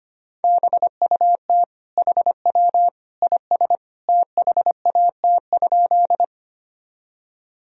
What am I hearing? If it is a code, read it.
Morse code, heard as a keyed tone, about 25 wpm, 700 Hz. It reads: BUT HW IS THAT?
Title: Sentences from Top 100 Words and Abbreviations 25wpm